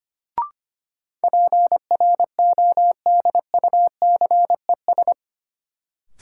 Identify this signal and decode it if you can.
Morse code, heard as a keyed tone, about 25 wpm, 700 Hz. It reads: PRODUCES
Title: Sentences from Top 1000 Words - Review Part 6 - New Word Drill - 25wpm